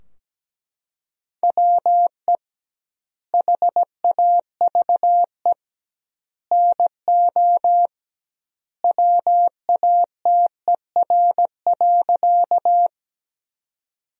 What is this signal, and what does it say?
Morse code, heard as a keyed tone, about 17 wpm, 700 Hz. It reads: WE HAVE NO WATER.
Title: 2x Word-Spacing - Sentences from Top 100 Words - 17wpm